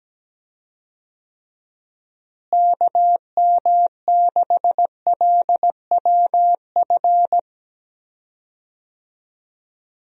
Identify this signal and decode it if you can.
Morse code, heard as a keyed tone, about 17 wpm, 700 Hz. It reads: KM6LWF
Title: USA Call Signs - 6 Characters - Repeated 2x - 17wpm